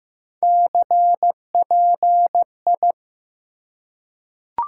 Morse code, heard as a keyed tone, about 15 wpm, 700 Hz. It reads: CPI
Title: Top 90 QSO Elements 15wpm